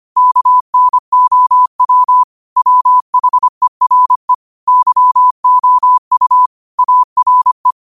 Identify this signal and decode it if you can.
Morse code, heard as a keyed tone, about 25 wpm, 1000 Hz. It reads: KNOW WHERE YOU ARE